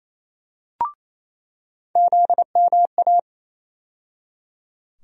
Morse code, heard as a keyed tone, about 28 wpm, 700 Hz. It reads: ZMA
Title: USA Call Sign Postfixes - 3 Letters 28wpm